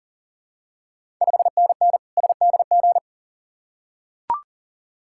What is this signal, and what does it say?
Morse code, heard as a keyed tone, about 42 wpm, 690 Hz. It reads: 5NN SDG
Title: Ontario QSO Party Exchange 40wpm